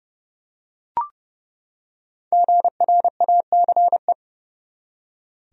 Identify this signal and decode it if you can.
Morse code, heard as a keyed tone, about 31 wpm, 700 Hz. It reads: GRACE
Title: USA Common Names - 30wpm